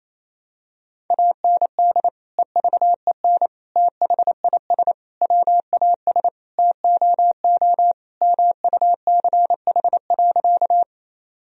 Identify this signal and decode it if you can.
Morse code, heard as a keyed tone, about 28 wpm, 700 Hz. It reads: AND EVEN THIS WAS TOO MUCH.
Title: Sentences from Top 1000 Words - Review Part 1 - 28wpm